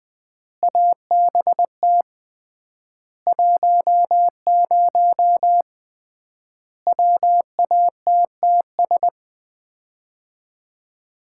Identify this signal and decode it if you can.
Morse code, heard as a keyed tone, about 20 wpm, 700 Hz. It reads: ABT 10 WATTS
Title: Common QSO Phrases - 3x Word-Spacing - 20wpm